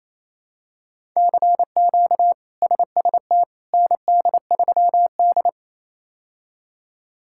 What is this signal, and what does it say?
Morse code, heard as a keyed tone, about 28 wpm, 700 Hz. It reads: CQ SST ND3D
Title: K1USN SST QSO Practice - 28wpm